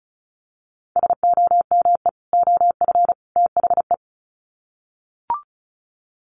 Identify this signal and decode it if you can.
Morse code, heard as a keyed tone, about 34 wpm, 700 Hz. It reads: SOME OF THE
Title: Sets of 3 Words 35wpm